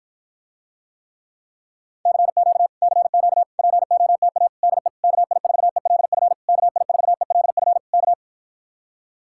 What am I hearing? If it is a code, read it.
Morse code, heard as a keyed tone, about 52 wpm, 700 Hz. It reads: CQ CQ POTA DE KI4ELF KI4ELF K